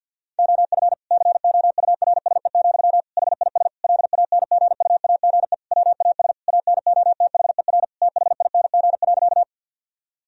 Morse code, heard as a keyed tone, about 50 wpm, 700 Hz. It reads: OF COURSE, HIS LANGUAGE WAS ANOTHER THING.
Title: Sentences from Top 1000 Words - Review Part 3 - 50wpm